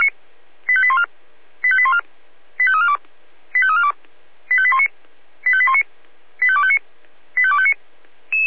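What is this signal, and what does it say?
Five-tone selective call (ZVEI), heard as quick runs of five tones, tones of about 70 ms. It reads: 85814 85814 85342 85342 85819 85819 85259 85259
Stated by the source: ZVEI five-tone call 85340 received on an SDR.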